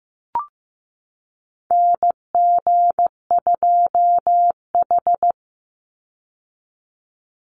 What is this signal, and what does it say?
Morse code, heard as a keyed tone, about 15 wpm, 700 Hz. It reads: NG2H
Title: Call Signs Repeated 2x - 15wpm